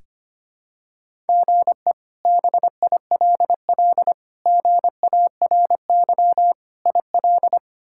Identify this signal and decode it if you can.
Morse code, heard as a keyed tone, about 25 wpm, 700 Hz. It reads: GE BILL GARY IL
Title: K1USN SST QSO Practice - 25wpm